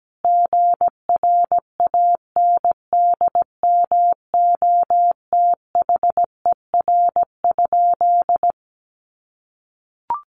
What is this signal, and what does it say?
Morse code, heard as a keyed tone, about 17 wpm, 700 Hz. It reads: GRANDMOTHER?
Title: Sentences from Top 1200 Words 17wpm